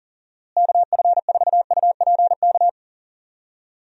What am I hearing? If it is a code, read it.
Morse code, heard as a keyed tone, about 40 wpm, 700 Hz. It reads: KF4UPK
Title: USA Call Signs - 6 Characters - Repeated 3x - 40wpm